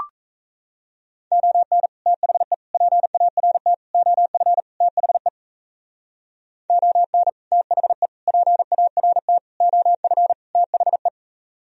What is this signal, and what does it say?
Morse code, heard as a keyed tone, about 33 wpm, 700 Hz. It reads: ON THE PART OF THE ON THE PART OF THE
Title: Speed-Racing - Sets of 5 Words 28wpm